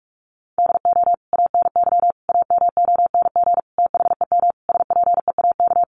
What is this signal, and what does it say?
Morse code, heard as a keyed tone, about 46 wpm, 700 Hz. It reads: DO ANY AMONG THEM SPEAK
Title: Sentences from Top 1100 Words 45wpm